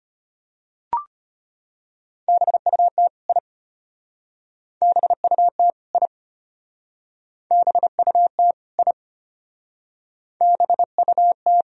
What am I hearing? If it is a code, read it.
Morse code, heard as a keyed tone, about 31 wpm, 700 Hz. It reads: BUT I BUT I BUT I BUT
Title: Speed-Racing - Sets of 2 Words 25wpm